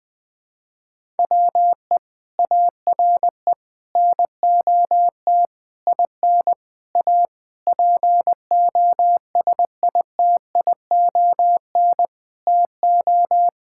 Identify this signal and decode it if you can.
Morse code, heard as a keyed tone, about 20 wpm, 700 Hz. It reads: WE ARE NOT IN A POSITION TO
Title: Sentences with 7-Word Phrases - 20wpm